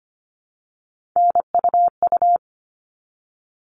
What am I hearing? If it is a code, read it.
Morse code, heard as a keyed tone, about 25 wpm, 700 Hz. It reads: NUU
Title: All Cut Numbers - 3-Digit 25wpm